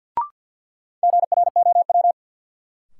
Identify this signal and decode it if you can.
Morse code, heard as a keyed tone, about 53 wpm, 700 Hz. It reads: GROW
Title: Sentences from Top 300 Words - New Word Drill - 50wpm